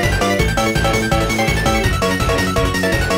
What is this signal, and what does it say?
There is no code.